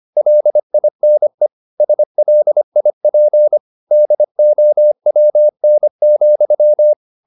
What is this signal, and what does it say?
Morse code, heard as a keyed tone, about 25 wpm, 600 Hz. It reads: LINE SLIP DOWN,
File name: the old man and the sea part 1 25wpm